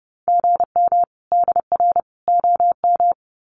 Morse code, heard as a keyed tone, about 28 wpm, 700 Hz. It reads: GM DR OM
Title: Common QSO Phrases 30wpm